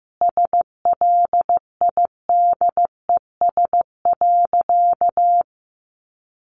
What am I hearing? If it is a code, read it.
Morse code, heard as a keyed tone, about 15 wpm, 700 Hz. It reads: SLIDES.
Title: Easy Phrases 15wpm